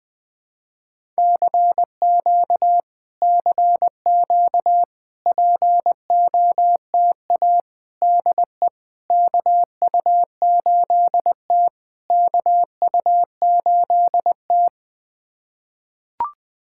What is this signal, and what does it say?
Morse code, heard as a keyed tone, about 20 wpm, 690 Hz. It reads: CQ CQ POTA DE KU8T KU8T
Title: POTA QSO Practice - 20wpm